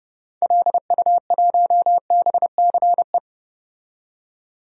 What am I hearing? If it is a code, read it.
Morse code, heard as a keyed tone, about 30 wpm, 700 Hz. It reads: LU1BCE
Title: DXCC-100 Call Signs - 6 Characters - 30wpm